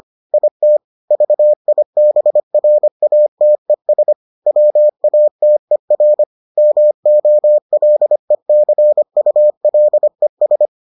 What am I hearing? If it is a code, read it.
Morse code, heard as a keyed tone, about 25 wpm, 600 Hz. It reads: IT VIBRATES WATER MOLECULES